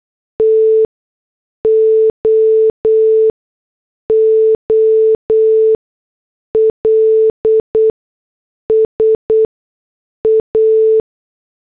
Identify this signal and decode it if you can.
Morse code, heard as a keyed tone, about 8 wpm, spaced as for 6 wpm, 440 Hz. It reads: TOOLS A